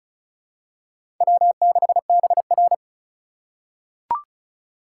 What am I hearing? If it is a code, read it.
Morse code, heard as a keyed tone, about 34 wpm, 700 Hz. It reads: W6BR
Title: USA Call Signs - 4 Characters - 8x Character Spacing - 35wpm